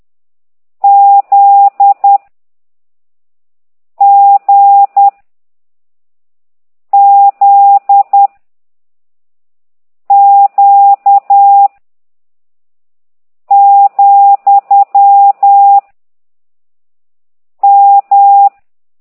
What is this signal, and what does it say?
Morse code, heard as a keyed tone, about 10 wpm, 800 Hz. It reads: Z G Z Q , M